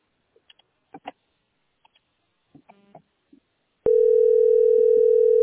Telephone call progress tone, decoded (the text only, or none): ringback tone